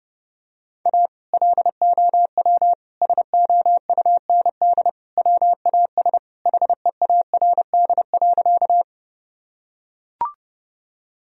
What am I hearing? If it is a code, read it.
Morse code, heard as a keyed tone, about 30 wpm, 700 Hz. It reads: A LOW SOUND WAS HEARD.
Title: Sentences from Top 1000 Words - Review Part 1 - 30wpm